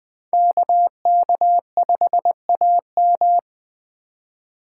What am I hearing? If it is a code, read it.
Morse code, heard as a keyed tone, about 20 wpm, 700 Hz. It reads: KK5AM
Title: Speed Racing - USA Call Signs 17wpm